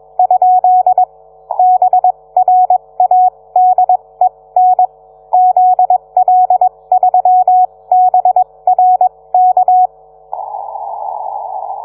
Morse code, heard as a keyed tone, about 21 wpm, spaced as for 18 wpm, 730 Hz. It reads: ? BRADEN ZL3BRK